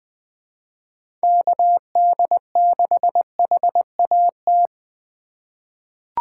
Morse code, heard as a keyed tone, about 20 wpm, 700 Hz. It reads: KD6HAT
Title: USA Call Signs - 4x Character Spacing - 20wpm